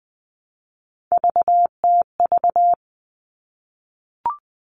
Morse code, heard as a keyed tone, about 20 wpm, 700 Hz. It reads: VTV
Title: All Cut Numbers - 3-Digit 20wpm